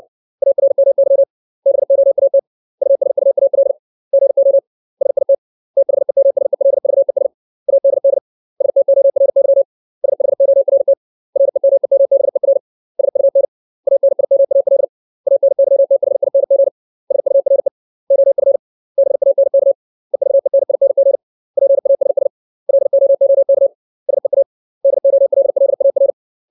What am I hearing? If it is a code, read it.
Morse code, heard as a keyed tone, about 60 wpm, 550 Hz. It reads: ARMY BOAT USUAL GO SIT THEMSELVES ADD STORY SHORT REMEMBER SUN ANIMAL ANYTHING SURE OF BANK EVENING PASS LOOK SEA DOLLAR
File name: 632 words / 632 Words 60WPM